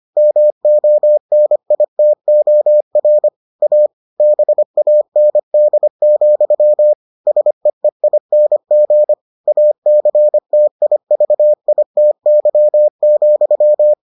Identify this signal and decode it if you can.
Morse code, heard as a keyed tone, about 25 wpm, 600 Hz. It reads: MONITOR A BAND, SEEING ACTIVITY,